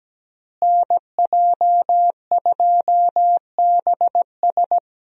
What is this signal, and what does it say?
Morse code, heard as a keyed tone, about 17 wpm, 700 Hz. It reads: NJ2BS
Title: USA Call Signs - 5 Characters - Repeated 3x - 17wpm